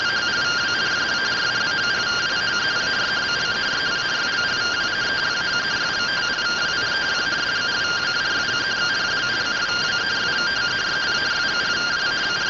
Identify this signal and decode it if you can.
Two-tone teleprinter signal, about 170 Hz shift.